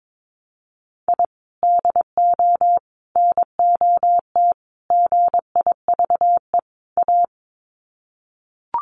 Morse code, heard as a keyed tone, about 22 wpm, 700 Hz. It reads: I DO NOT GIVE A
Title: Sets of 5 Words Encore2 22wpm